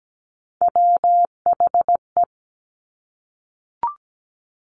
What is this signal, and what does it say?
Morse code, heard as a keyed tone, about 17 wpm, 700 Hz. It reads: WHE
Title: Top Three Letter Combinations in Words 17wpm